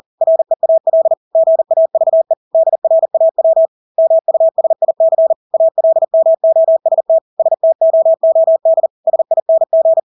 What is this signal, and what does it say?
Morse code, heard as a keyed tone, about 40 wpm, 650 Hz. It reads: REAL GAVE DRAW MUSIC ALMOST STOOD SING
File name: Engelske ord 40-40 wpm